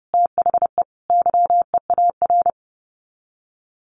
Morse code, heard as a keyed tone, about 30 wpm, 700 Hz. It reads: THE YEAR
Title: Sets of 5 Words Encore 30wpm